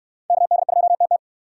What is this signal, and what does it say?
Morse code, heard as a keyed tone, about 66 wpm, 700 Hz. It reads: KD2TT